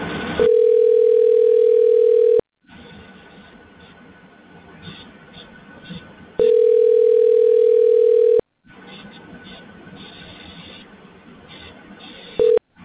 Ringback tone (North America), 440 + 480 Hz.